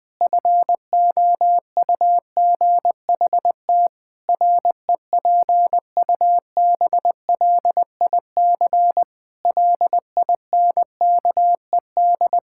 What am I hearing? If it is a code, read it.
Morse code, heard as a keyed tone, about 20 wpm, 700 Hz. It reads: FOUGHT REPUBLIC LINKED
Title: Pangram Challenge v2 - 20wpm